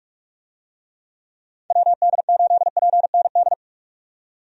Morse code, heard as a keyed tone, about 44 wpm, 700 Hz. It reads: WD8PND
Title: USA Call Signs - 6 Characters - 45wpm